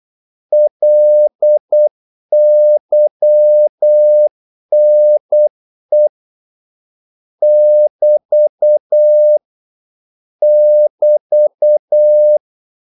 Morse code, heard as a keyed tone, about 8 wpm, 600 Hz. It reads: LYNE <BT> <BT>
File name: Korte ord DK 8wpm